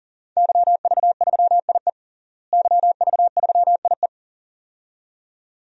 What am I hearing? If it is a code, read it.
Morse code, heard as a keyed tone, about 40 wpm, 700 Hz. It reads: YV3IE YV3IE
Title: DXCC-100 Call Signs - 5 Characters - Repeated 2x - 40wpm